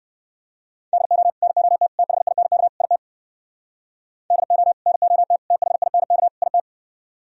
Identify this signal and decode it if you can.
Morse code, heard as a keyed tone, about 58 wpm, 700 Hz. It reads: DO NOT THINK IT DO NOT THINK IT